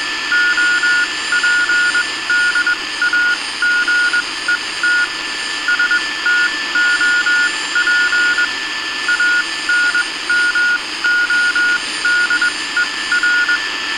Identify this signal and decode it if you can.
Morse code, heard as a keyed tone, about 21 wpm, spaced as for 17 wpm, 1440 Hz. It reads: OPDAGET STOP ANMODER